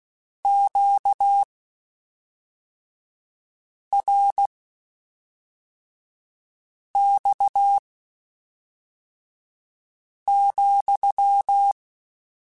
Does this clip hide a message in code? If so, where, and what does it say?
Yes: Morse code from 0.4 to 11.7 s, heard as a keyed tone, about 16 wpm, spaced as for 3 wpm, 800 Hz. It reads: QRX,